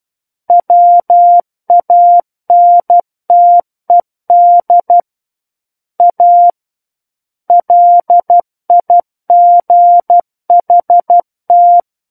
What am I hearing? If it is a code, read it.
Morse code, heard as a keyed tone, about 12 wpm, 700 Hz. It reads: WANTED A LIGHT
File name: Book One / 03-Chapter 3 - 12 WPM